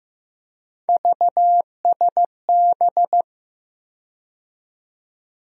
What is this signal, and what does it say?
Morse code, heard as a keyed tone, about 15 wpm, 700 Hz. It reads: VSB
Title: USA Call Sign Postfixes - 3 Letters 15wpm